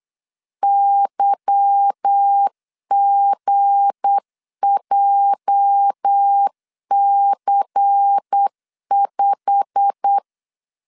Morse code, heard as a keyed tone, about 8 wpm, 800 Hz. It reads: YGJC5
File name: LEZIONI / 10 SIGNS / 10A SLASH 8wpm 3452